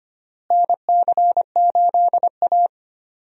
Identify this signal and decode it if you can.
Morse code, heard as a keyed tone, about 25 wpm, 700 Hz. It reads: NC8A
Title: USA Call Signs - 4 Characters - Repeated 3x - 25wpm